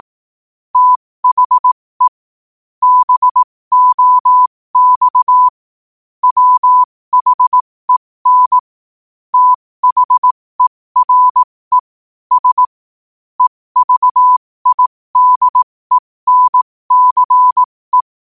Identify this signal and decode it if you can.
Morse code, heard as a keyed tone, about 18 wpm, spaced as for 15 wpm, 1000 Hz. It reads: THE BOX WHEN THERE S EVIDENCE
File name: qotdfiles / 15wpm